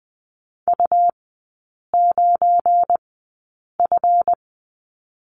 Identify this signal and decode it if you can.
Morse code, heard as a keyed tone, about 20 wpm, spaced as for 8 wpm, 700 Hz. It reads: U9F
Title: 172 Lesson 28e - Number 9 - Three Characters 20wpm